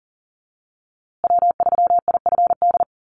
Morse code, heard as a keyed tone, about 40 wpm, 700 Hz. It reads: W3IFD
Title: USA Call Signs Repeated 2x 40wpm